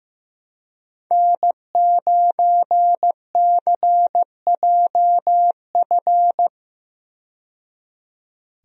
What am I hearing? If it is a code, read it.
Morse code, heard as a keyed tone, about 15 wpm, 700 Hz. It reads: N9CJF
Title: USA Call Signs - 5 Characters - 15wpm